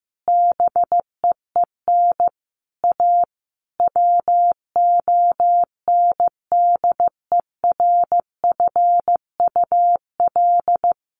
Morse code, heard as a keyed tone, about 15 wpm, 700 Hz. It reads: BEEN A WONDERFUL
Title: Sentences from Top 1100 Words 15wpm